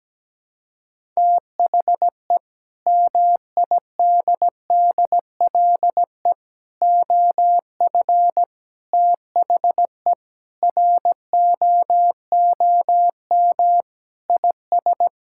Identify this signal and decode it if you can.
Morse code, heard as a keyed tone, about 17 wpm, 700 Hz. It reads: THE MIDDLE OF THE ROOM IS
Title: Sentences from Top 1000 Words - Review Part 6 - 17wpm